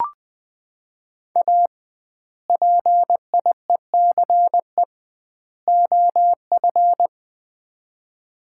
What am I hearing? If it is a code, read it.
Morse code, heard as a keyed tone, about 20 wpm, 700 Hz. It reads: A PIECE OF
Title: 2x Word-Spacing - Sets of 3 Words - 20wpm